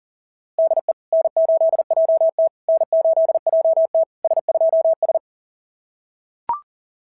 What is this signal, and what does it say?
Morse code, heard as a keyed tone, about 40 wpm, 660 Hz. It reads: DE N8JT N8JT S2S
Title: SOTA QSO Practice 40wpm